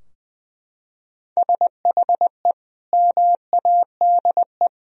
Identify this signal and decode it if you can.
Morse code, heard as a keyed tone, about 21 wpm, 700 Hz. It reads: SHE MADE
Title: Sentences from Top 1000 Words - Review Part 4 - 20wpm